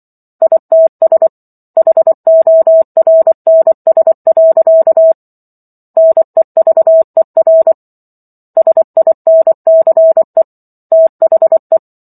Morse code, heard as a keyed tone, about 24 wpm, 650 Hz. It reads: ITS HORNS. NEVER SINCE THE